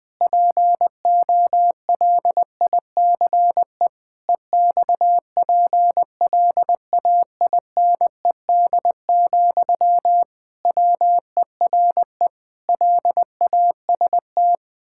Morse code, heard as a keyed tone, about 21 wpm, 700 Hz. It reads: POLICE EXPLAINED, WERE LAST
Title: Pangram Challenge v2 - 20wpm